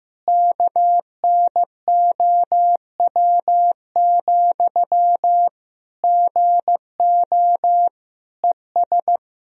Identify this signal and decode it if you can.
Morse code, heard as a keyed tone, about 15 wpm, 700 Hz. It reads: KNOW, GO ES
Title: Sentences from Top 200 Words and Abbreviations 15wpm